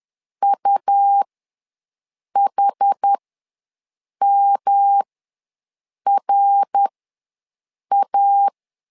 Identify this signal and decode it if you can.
Morse code, heard as a keyed tone, about 10 wpm, spaced as for 6 wpm, 800 Hz. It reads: UHMRA
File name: LEZIONI / 3 MWRO / 3 MWRO ws 10wpm 2310